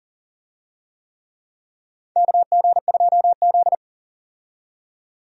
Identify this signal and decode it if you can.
Morse code, heard as a keyed tone, about 40 wpm, 700 Hz. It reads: KG2Z
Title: USA Call Signs - 4 Characters - 40wpm